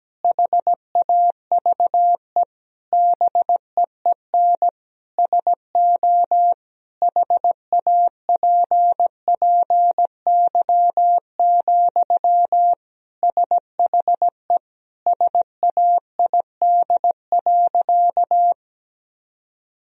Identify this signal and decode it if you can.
Morse code, heard as a keyed tone, about 17 wpm, 700 Hz. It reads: HAVE BEEN SO HAPPY, SHE SAID.
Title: Sentences from Top 1000 Words - Review Part 1 - 17wpm